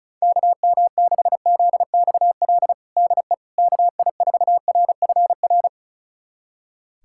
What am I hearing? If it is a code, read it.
Morse code, heard as a keyed tone, about 36 wpm, 700 Hz. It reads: KM6ZXL DE KI4RFR
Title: USA Callsign DE Callsign 35wpm